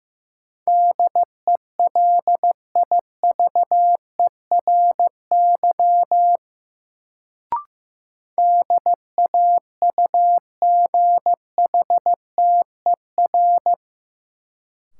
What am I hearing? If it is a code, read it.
Morse code, heard as a keyed tone, about 15 wpm, 700 Hz. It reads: DELIVERY DAUGHTER
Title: Sentences from Top 1500 Words - New Word Drill - 15wpm